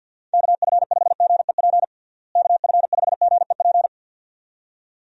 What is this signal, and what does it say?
Morse code, heard as a keyed tone, about 51 wpm, 700 Hz. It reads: KF5GEP KF5GEP
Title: USA Call Signs - 6 Characters - Repeated 2x - 50wpm